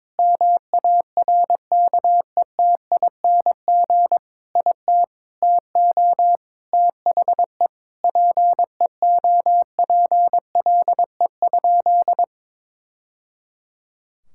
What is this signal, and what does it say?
Morse code, heard as a keyed tone, about 22 wpm, 700 Hz. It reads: MARKETING IT TO THE PEOPLE?
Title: Sentences from Top 1000 Words - Review Part 5 - 22wpm